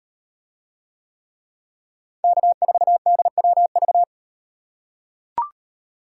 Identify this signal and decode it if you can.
Morse code, heard as a keyed tone, about 38 wpm, 700 Hz. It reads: K4DWV